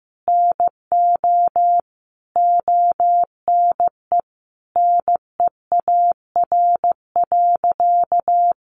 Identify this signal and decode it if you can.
Morse code, heard as a keyed tone, about 15 wpm, 700 Hz. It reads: NO ONE NEAR.
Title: Sentences from Top 200 Words 15wpm